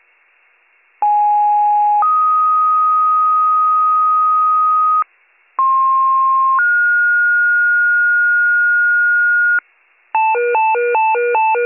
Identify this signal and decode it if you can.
Two-tone page, heard as a one-second tone, then a three-second tone, about 830 then 1250 Hz, about 1060 then 1430 Hz, with noise. It is heard from 1.0 to 9.6 s.